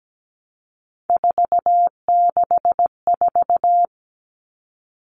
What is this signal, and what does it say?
Morse code, heard as a keyed tone, about 17 wpm, 700 Hz. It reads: 464
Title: Numbers - 3-Digit 17wpm